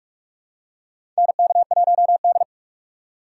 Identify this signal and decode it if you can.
Morse code, heard as a keyed tone, about 46 wpm, 700 Hz. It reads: NK1D
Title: USA Call Signs - 4 Characters - 4x Character Spacing - 45wpm